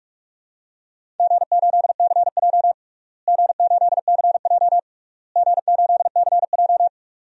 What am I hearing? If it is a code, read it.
Morse code, heard as a keyed tone, about 41 wpm, 700 Hz. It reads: G8CJ G8CJ G8CJ
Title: DXCC-100 Call Signs - 4 Characters - Repeated 3x - 45wpm